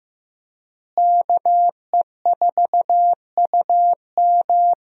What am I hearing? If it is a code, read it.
Morse code, heard as a keyed tone, about 15 wpm, 700 Hz. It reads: KE4UM
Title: USA Call Signs - 5 Characters - Repeated 2x - 15wpm